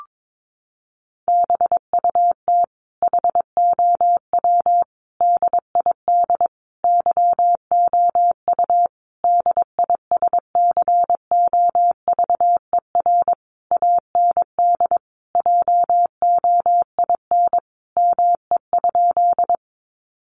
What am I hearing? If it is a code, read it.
Morse code, heard as a keyed tone, about 22 wpm, 700 Hz. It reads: BUT HOW DID YOU DISCOVER AND JOIN ME?
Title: Sentences from Top 1400 Words 22wpm